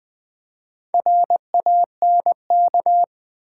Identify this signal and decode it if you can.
Morse code, heard as a keyed tone, about 20 wpm, 700 Hz. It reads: RANK
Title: Build A Word - 2 Letters At A Time - 4-Letter Words - 20wpm